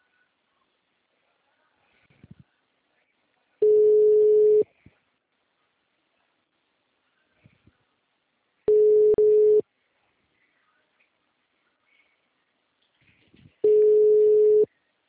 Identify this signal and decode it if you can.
Ringback tone (Europe, 425 Hz).
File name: out- -2024- - - .48508